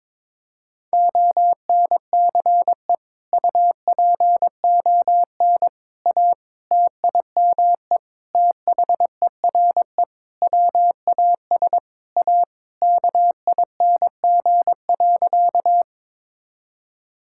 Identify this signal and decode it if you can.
Morse code, heard as a keyed tone, about 22 wpm, 700 Hz. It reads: ONCE UPON A TIME THERE WAS A KING.
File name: Sentences with 7-Word Phrases - 22wpm